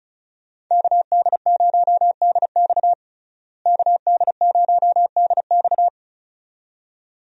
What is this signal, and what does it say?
Morse code, heard as a keyed tone, about 34 wpm, 700 Hz. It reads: KD0DX KD0DX
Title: USA Call Signs - 5 Characters - Repeated 2x - 35wpm